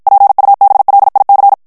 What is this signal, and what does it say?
Morse code, heard as a keyed tone, about 52 wpm, 800 Hz. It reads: PUBLIC
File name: FRANCAIS TEXTE CLAIR 2 VITESSE 50 MOTS MINUTES